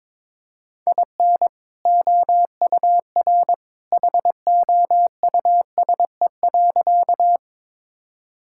Morse code, heard as a keyed tone, about 22 wpm, 700 Hz. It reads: IN OUR HOUSE.
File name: practice / 22wpm / Sentences with 2-Word Phrases - 22wpm